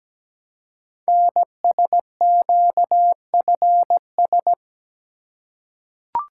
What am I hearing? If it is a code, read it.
Morse code, heard as a keyed tone, about 17 wpm, 700 Hz. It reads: NSQFS
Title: Code Groups - 5 Letters - 17wpm